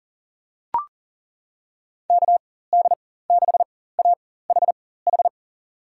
Morse code, heard as a keyed tone, about 40 wpm, spaced as for 19 wpm, 700 Hz. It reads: KD6AHH
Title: USA Call Signs - 6 Characters - 4x Character Spacing - 40wpm